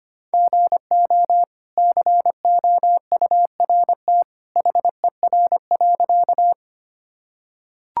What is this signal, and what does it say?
Morse code, heard as a keyed tone, about 25 wpm, 700 Hz. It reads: GO COURT HER.
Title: Sentences from Top 1600 Words 25wpm